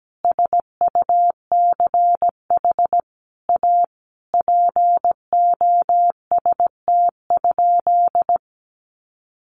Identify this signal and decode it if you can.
Morse code, heard as a keyed tone, about 17 wpm, 700 Hz. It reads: SUCH A POST?
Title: Sentences from Top 1000 Words 17wpm